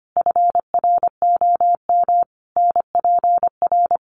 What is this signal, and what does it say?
Morse code, heard as a keyed tone, about 25 wpm, 700 Hz. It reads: FROM NPR
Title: Sets of 5 Words 25wpm